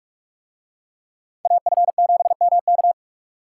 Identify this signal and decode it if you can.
Morse code, heard as a keyed tone, about 43 wpm, 700 Hz. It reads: AF7MK